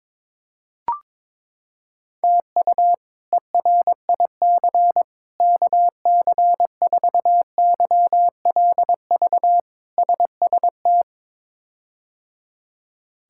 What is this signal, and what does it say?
Morse code, heard as a keyed tone, about 22 wpm, 700 Hz. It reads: TU ERIC KC4YLV SST
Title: K1USN SST QSO Practice - 22wpm